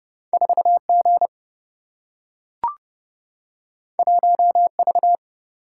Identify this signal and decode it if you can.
Morse code, heard as a keyed tone, about 29 wpm, 700 Hz. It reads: 4G 1V
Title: Single-Letter Single-Number 30wpm